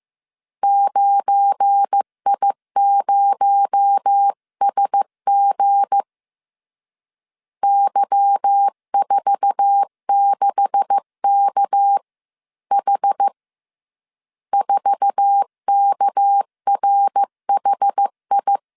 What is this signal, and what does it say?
Morse code, heard as a keyed tone, about 15 wpm, 800 Hz. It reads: 9I0SG Y46KH 4KRHI
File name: CW / DET / 1 9 1500C S A 14wpm 3222